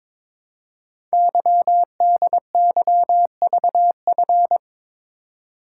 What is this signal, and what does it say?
Morse code, heard as a keyed tone, about 22 wpm, 690 Hz. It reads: YDYVF
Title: Code Groups - 5 Letters - 22wpm